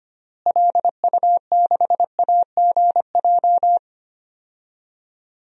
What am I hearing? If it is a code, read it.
Morse code, heard as a keyed tone, about 25 wpm, 690 Hz. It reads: LU6AGJ
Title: DXCC-100 Call Signs - 6 Characters - Repeated 3x - 25wpm